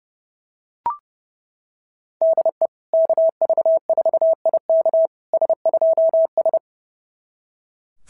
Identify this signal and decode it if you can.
Morse code, heard as a keyed tone, about 30 wpm, 690 Hz. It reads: DE KV4IK S2S
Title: SOTA QSO Practice 30wpm